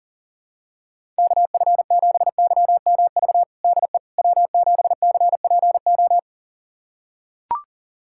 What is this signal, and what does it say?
Morse code, heard as a keyed tone, about 37 wpm, 700 Hz. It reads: KF7YMV DE W7CPO
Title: USA Callsign DE Callsign 40wpm